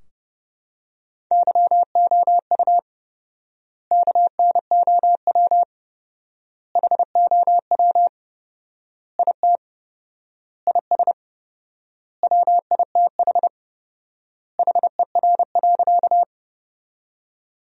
Morse code, heard as a keyed tone, about 32 wpm, 700 Hz. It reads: YOU KNOW HOW IT IS WITH HER.
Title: 4x Word-Spacing - Sentences from Top 100 Words - 30wpm